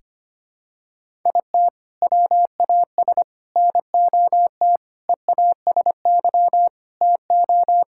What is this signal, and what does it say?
Morse code, heard as a keyed tone, about 25 wpm, 700 Hz. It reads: IT WAS NOT EASY TO
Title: Sentences from Top 1000 Words - Review Part 6 - 25wpm